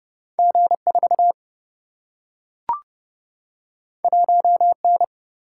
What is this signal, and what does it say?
Morse code, heard as a keyed tone, about 30 wpm, 700 Hz. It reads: G4 1N
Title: Single-Letter Single-Number 30wpm